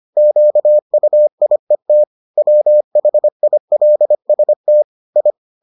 Morse code, heard as a keyed tone, about 25 wpm, 600 Hz. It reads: QUIET WHILST I